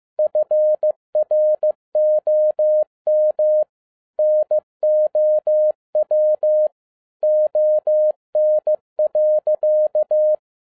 Morse code, heard as a keyed tone, about 15 wpm, 610 Hz. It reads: FROM NOW ON.
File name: EXCITING QRP CW OPERATION 15wpm